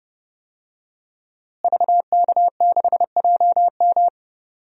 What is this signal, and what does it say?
Morse code, heard as a keyed tone, about 30 wpm, 700 Hz. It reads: VK6JM
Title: DXCC-100 Call Signs - 5 Characters - Repeated 2x - 30wpm